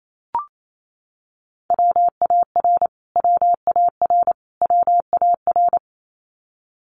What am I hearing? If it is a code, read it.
Morse code, heard as a keyed tone, about 28 wpm, 700 Hz. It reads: WAR WAR WAR
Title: Top 500 Words Repeated 28wpm